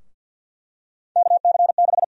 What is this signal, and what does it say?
Morse code, heard as a keyed tone, about 48 wpm, 700 Hz. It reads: KC6
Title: USA Call Sign Prefixes - 50wpm